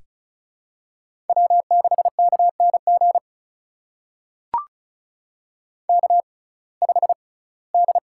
Morse code, heard as a keyed tone, about 36 wpm, spaced as for 22 wpm, 700 Hz. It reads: W6KNG K5D
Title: USA Call Signs - 6x Character Spacing - 35wpm